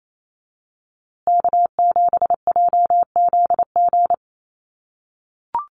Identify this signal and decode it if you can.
Morse code, heard as a keyed tone, about 28 wpm, 700 Hz. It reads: K7JZG
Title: USA Call Signs Repeated 3x 28wpm